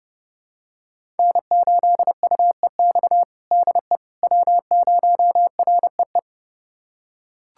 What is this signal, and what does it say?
Morse code, heard as a keyed tone, about 30 wpm, 700 Hz. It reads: N8UEX DE W0REE
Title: USA Callsign DE Callsign - 5 Characters - 30wpm